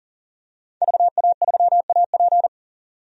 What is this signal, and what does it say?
Morse code, heard as a keyed tone, about 40 wpm, 700 Hz. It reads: VA3AP
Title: DXCC-100 Call Signs - 5 Characters - 40wpm